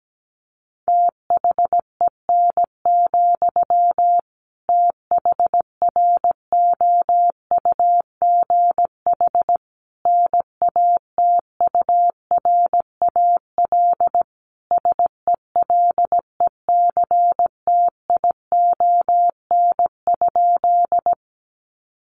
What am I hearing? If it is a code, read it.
Morse code, heard as a keyed tone, about 17 wpm, 700 Hz. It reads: THEN, THROUGH NATURAL SELECTION?
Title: Sentences from Top 1400 Words 17wpm